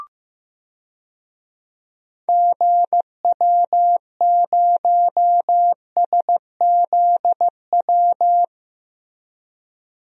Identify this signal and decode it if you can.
Morse code, heard as a keyed tone, about 15 wpm, 700 Hz. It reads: GW0SZW
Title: DXCC-100 Call Signs - 6 Characters - 15wpm